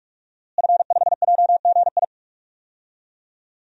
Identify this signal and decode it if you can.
Morse code, heard as a keyed tone, about 41 wpm, 700 Hz. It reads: F5JGI
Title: DXCC-100 Call Signs - 5 Characters - Repeated 2x - 45wpm